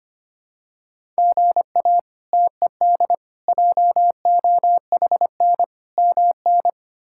Morse code, heard as a keyed tone, about 25 wpm, 700 Hz. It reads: GA TED JOHN MN
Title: K1USN SST QSO Practice - 25wpm